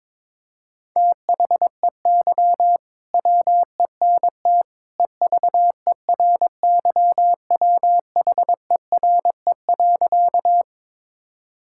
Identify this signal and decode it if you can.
Morse code, heard as a keyed tone, about 22 wpm, 700 Hz. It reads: THEY WENT EVERYWHERE.
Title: Sentences from Top 1600 Words 22wpm